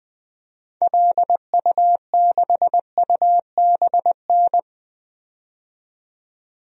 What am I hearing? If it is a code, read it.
Morse code, heard as a keyed tone, about 20 wpm, 700 Hz. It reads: LU6UBN
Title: DXCC-100 Call Signs - 6 Characters - 20wpm